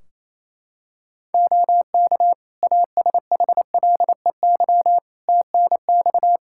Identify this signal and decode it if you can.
Morse code, heard as a keyed tone, about 28 wpm, 700 Hz. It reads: OK ASHLEY TNX
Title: Practice QSO - Exchange 4 - Ending - 28wpm